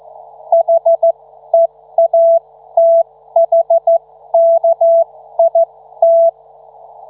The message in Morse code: HEATHKIT